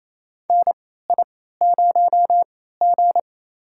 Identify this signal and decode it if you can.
Morse code, heard as a keyed tone, about 28 wpm, spaced as for 16 wpm, 700 Hz. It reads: NI0G
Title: USA Call Signs - 4 Characters - 3x Character Spacing - 28wpm